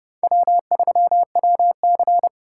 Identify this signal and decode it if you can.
Morse code, heard as a keyed tone, about 30 wpm, 700 Hz. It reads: W3WC